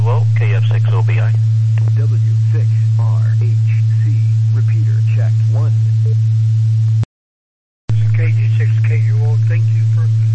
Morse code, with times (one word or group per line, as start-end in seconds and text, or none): none